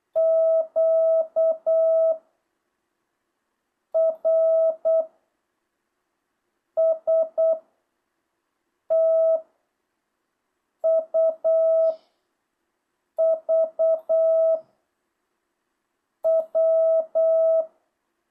Morse code, heard as a keyed tone, about 8 wpm, 660 Hz. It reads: Q R S T U V W
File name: jh8cbh